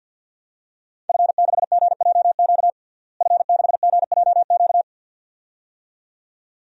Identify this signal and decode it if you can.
Morse code, heard as a keyed tone, about 51 wpm, 700 Hz. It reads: F6GJQ F6GJQ
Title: DXCC-100 Call Signs - 5 Characters - Repeated 2x - 50wpm